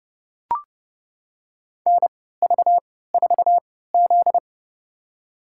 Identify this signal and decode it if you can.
Morse code, heard as a keyed tone, about 30 wpm, spaced as for 17 wpm, 700 Hz. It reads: NV4Z
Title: USA Call Signs - 3x Character Spacing - 30wpm